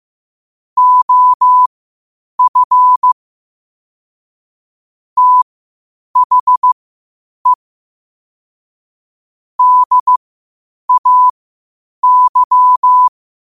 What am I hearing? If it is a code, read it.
Morse code, heard as a keyed tone, about 15 wpm, spaced as for 8 wpm, 1000 Hz. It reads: OF THE DAY
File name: qotdfiles / 8wpm